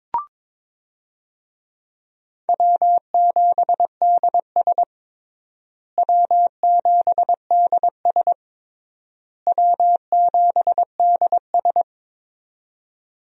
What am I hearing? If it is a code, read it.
Morse code, heard as a keyed tone, about 22 wpm, 700 Hz. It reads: W7DS W7DS W7DS
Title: USA Call Signs - 4 Characters - Repeated 3x - 22wpm